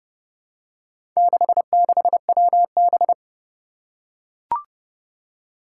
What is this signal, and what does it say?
Morse code, heard as a keyed tone, about 32 wpm, 700 Hz. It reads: 66WB
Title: Four Challenging Letter-Number Combinations 30wpm